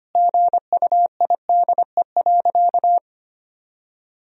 Morse code, heard as a keyed tone, about 25 wpm, 700 Hz. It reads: GUIDE.
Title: Sentences from Top 1000 Words - Review Part 3 - 25wpm